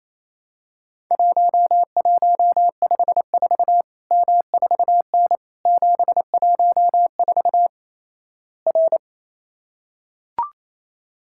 Morse code, heard as a keyed tone, about 28 wpm, 700 Hz. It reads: 1154 M4N 714 R
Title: Worked All Europe DX Contest - QTC Exchange - 28wpm